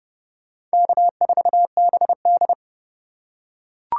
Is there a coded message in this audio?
Yes: Morse code from 0.7 to 2.5 s, heard as a keyed tone, about 31 wpm, 700 Hz. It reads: K4BD